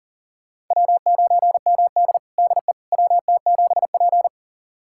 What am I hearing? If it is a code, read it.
Morse code, heard as a keyed tone, about 40 wpm, 700 Hz. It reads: W9MD DE WT7P
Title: USA Callsign DE Callsign - 4 Characters - 40wpm